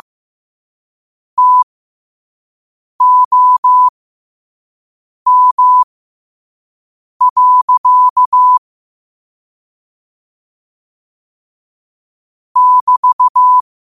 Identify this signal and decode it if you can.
Morse code, heard as a keyed tone, about 15 wpm, spaced as for 5 wpm, 1000 Hz. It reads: TOM. <BT>
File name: qotdfiles / 5wpm